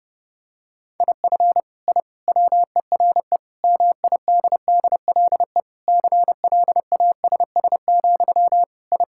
Morse code, heard as a keyed tone, about 30 wpm, 700 Hz. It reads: IF I WERE MIDDLE CLASS, I
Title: Sentences from Top 1000 Words - Review Part 6 - 30wpm